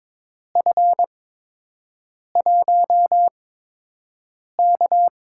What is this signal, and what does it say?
Morse code, heard as a keyed tone, about 22 wpm, spaced as for 6 wpm, 700 Hz. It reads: F1K
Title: USA Call Signs - 8x Character Spacing - 22wpm